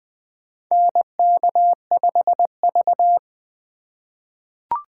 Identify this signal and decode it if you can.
Morse code, heard as a keyed tone, about 20 wpm, 700 Hz. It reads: NK5V
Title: USA Call Signs - 4 Characters - 4x Character Spacing - 20wpm